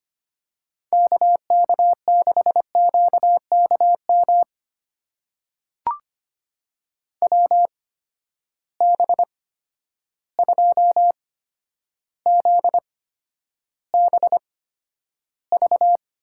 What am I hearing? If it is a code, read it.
Morse code, heard as a keyed tone, about 25 wpm, spaced as for 11 wpm, 700 Hz. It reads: KK6QKM WB2ZBV
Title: USA Call Signs - 8x Character Spacing - 25wpm